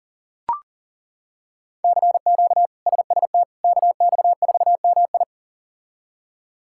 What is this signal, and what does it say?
Morse code, heard as a keyed tone, about 37 wpm, 700 Hz. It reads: CQ SST KX4MI
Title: K1USN SST QSO Practice - 40wpm